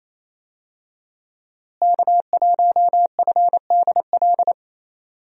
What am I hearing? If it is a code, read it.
Morse code, heard as a keyed tone, about 28 wpm, 700 Hz. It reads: K1FDL
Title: USA Call Signs - 5 Characters - Repeated 2x - 28wpm